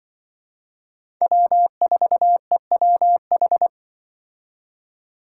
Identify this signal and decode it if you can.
Morse code, heard as a keyed tone, about 24 wpm, 700 Hz. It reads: W4EWH
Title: Speed Racing - USA 1x3 Call Signs 20wpm